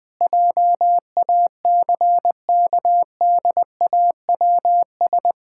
Morse code, heard as a keyed tone, about 20 wpm, 700 Hz. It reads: JACKDAWS
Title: Pangram Challenge v2 - 20wpm